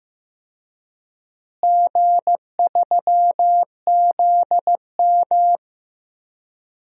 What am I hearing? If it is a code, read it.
Morse code, heard as a keyed tone, about 15 wpm, 700 Hz. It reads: G3ZM
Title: DXCC-100 Call Signs - 4 Characters - Repeated 3x - 15wpm